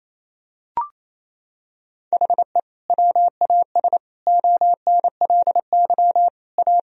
Morse code, heard as a keyed tone, about 28 wpm, 700 Hz. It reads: HE WAS ONLY A
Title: Sentences from Top 1000 Words - Review Part 4 - 28wpm